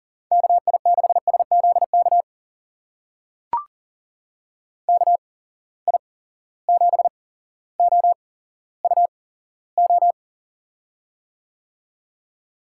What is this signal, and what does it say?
Morse code, heard as a keyed tone, about 40 wpm, spaced as for 17 wpm, 700 Hz. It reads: KI6SZK KI7OUO
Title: USA Call Signs - 6 Characters - 8x Character Spacing - 40wpm